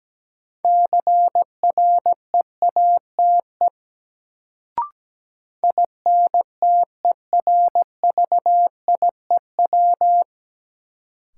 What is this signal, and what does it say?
Morse code, heard as a keyed tone, about 17 wpm, 700 Hz. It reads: CREATE INTERVIEW
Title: Sentences from Top 1600 Words - New Word Drill Alt - 17wpm